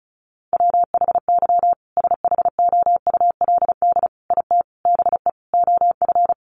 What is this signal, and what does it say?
Morse code, heard as a keyed tone, about 34 wpm, 700 Hz. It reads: WHY SHOULD IT BE OF
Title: Sentences from Top 1200 Words 35wpm